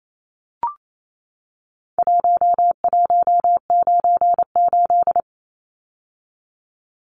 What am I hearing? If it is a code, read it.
Morse code, heard as a keyed tone, about 28 wpm, 700 Hz. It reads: 1198
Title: Numbers - Contest Style - 4-Digit 28wpm